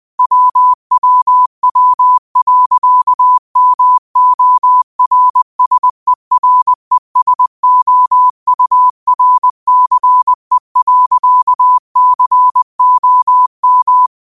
Morse code, heard as a keyed tone, about 20 wpm, 1000 Hz. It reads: WWW.MORSERESOURCE.COM
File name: qotdfiles / 20wpm